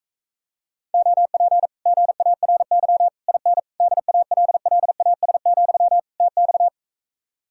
Morse code, heard as a keyed tone, about 43 wpm, 700 Hz. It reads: OP GARY IN DALLAS, TX